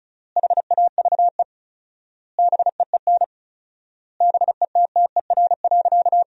Morse code, heard as a keyed tone, about 34 wpm, 700 Hz. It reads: HAVE BEEN BETTER.
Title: 4x Word-Spacing - Sentences from Top 500 Words - 35wpm